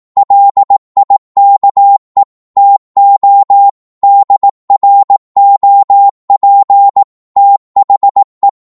Morse code, heard as a keyed tone, about 18 wpm, 800 Hz. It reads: LIKE TO DROP THE